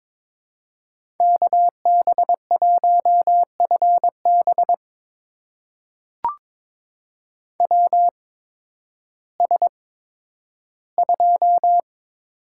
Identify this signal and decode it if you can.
Morse code, heard as a keyed tone, about 22 wpm, spaced as for 12 wpm, 700 Hz. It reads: KB1FB WS2
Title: USA Call Signs - 8x Character Spacing - 22wpm